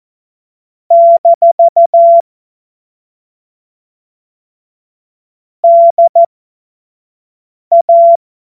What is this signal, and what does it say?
Morse code, heard as a keyed tone, about 14 wpm, spaced as for 5 wpm, 680 Hz. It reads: - DA
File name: Morsekurs / Witze / Witz 12. Oktober 2021, 25 F70